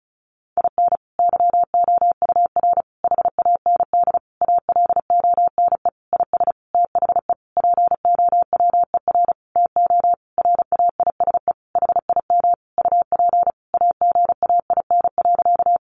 Morse code, heard as a keyed tone, about 34 wpm, 700 Hz. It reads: IN YOUR HAND ALONE IS THE POWER TO RAISE HIM UP AGAIN.
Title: Sentences from Top 1400 Words 35wpm